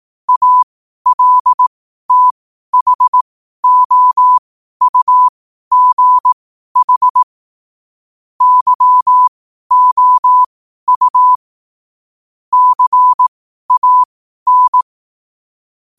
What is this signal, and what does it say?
Morse code, heard as a keyed tone, about 18 wpm, spaced as for 12 wpm, 1000 Hz. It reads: ALTHOUGH YOU CAN
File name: qotdfiles / 12wpm